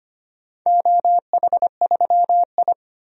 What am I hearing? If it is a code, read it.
Morse code, heard as a keyed tone, about 25 wpm, 700 Hz. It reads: OH3I
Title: DXCC-100 Call Signs - 4 Characters - 25wpm